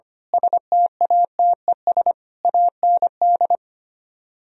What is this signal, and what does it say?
Morse code, heard as a keyed tone, about 25 wpm, 700 Hz. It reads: STATES AND
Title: Speed-Racing - Sets of 5 Words 25wpm